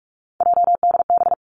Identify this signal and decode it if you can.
Morse code, heard as a keyed tone, about 46 wpm, 700 Hz. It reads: JDB
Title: Three Challenging Letter-Number Combinations 45wpm